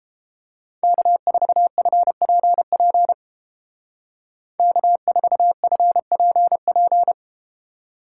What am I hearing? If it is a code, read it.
Morse code, heard as a keyed tone, about 30 wpm, 700 Hz. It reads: K4FPP K4FPP